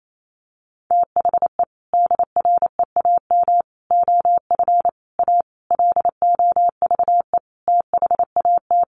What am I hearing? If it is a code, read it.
Morse code, heard as a keyed tone, about 28 wpm, 700 Hz. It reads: THE DREAM OF A LOVE THAT
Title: Sentences from Top 1500 Words 28wpm